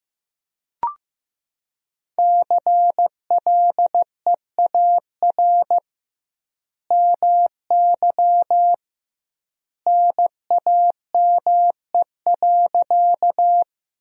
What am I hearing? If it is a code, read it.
Morse code, heard as a keyed tone, about 16 wpm, 700 Hz. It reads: CLEAR MY NAME.
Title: 2x Word-Spacing - Sentences from Top 500 Words - 15wpm